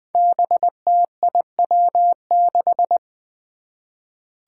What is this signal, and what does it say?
Morse code, heard as a keyed tone, about 20 wpm, 700 Hz. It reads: BTIW6
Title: Five Challenging Letter-Number Combinations 20wpm